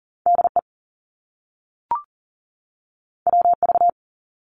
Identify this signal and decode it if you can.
Morse code, heard as a keyed tone, about 42 wpm, 700 Hz. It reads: DE WV
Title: US States 40wpm